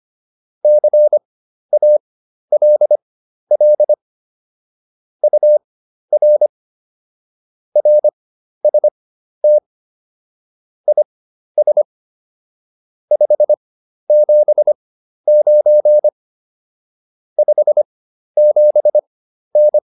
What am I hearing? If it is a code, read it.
Morse code, heard as a keyed tone, about 25 wpm, spaced as for 12 wpm, 600 Hz. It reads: CALL UR RST IS 579 57N